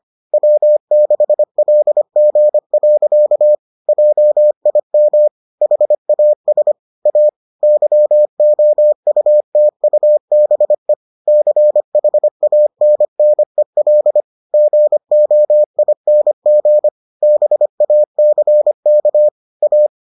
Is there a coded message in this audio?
Yes: Morse code from 0.3 to 19.9 s, heard as a keyed tone, about 25 wpm, 600 Hz. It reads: W6LG. JIM HAS A YOUTUBE CHANNEL GOING BACK A